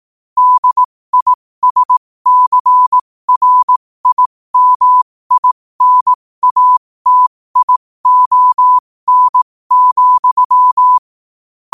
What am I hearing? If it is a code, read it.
Morse code, heard as a keyed tone, about 18 wpm, spaced as for 15 wpm, 1000 Hz. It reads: DISCRIMINATION,
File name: qotdfiles / 15wpm